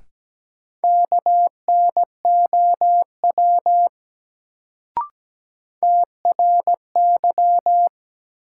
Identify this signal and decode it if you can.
Morse code, heard as a keyed tone, about 17 wpm, 700 Hz. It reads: KNOW TRY
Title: Sets of 2 Words - New Word Drill - 17wpm